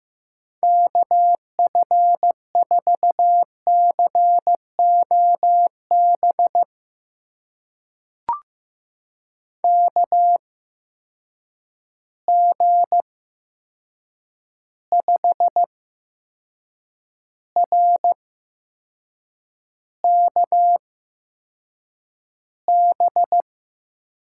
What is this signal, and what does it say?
Morse code, heard as a keyed tone, about 15 wpm, spaced as for 6 wpm, 700 Hz. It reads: KF4COB KG5RKB
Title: USA Call Signs - 6 Characters - 8x Character Spacing - 15wpm